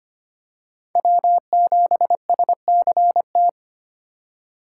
Morse code, heard as a keyed tone, about 25 wpm, 700 Hz. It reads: W7SCT
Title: USA Call Signs - 5 Characters - 3x Character Spacing - 25wpm